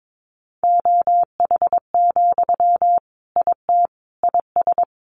Morse code, heard as a keyed tone, about 22 wpm, 700 Hz. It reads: OH, IT IS
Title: Sentences from Top 1000 Words 22wpm